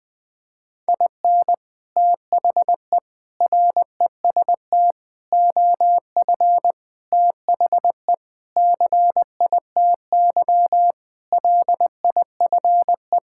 Morse code, heard as a keyed tone, about 20 wpm, 700 Hz. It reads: IN THE REST OF THE CITY LIFE
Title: Sentences with 5-Word Phrases - 20wpm